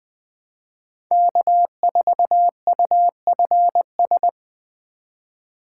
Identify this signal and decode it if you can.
Morse code, heard as a keyed tone, about 24 wpm, 700 Hz. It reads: K4UFS
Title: Speed Racing - USA 1x3 Call Signs 20wpm